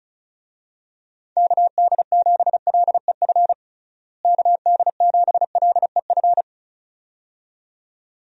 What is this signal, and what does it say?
Morse code, heard as a keyed tone, about 34 wpm, 700 Hz. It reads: KD7LEF KD7LEF
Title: USA Call Signs - 6 Characters - Repeated 2x - 35wpm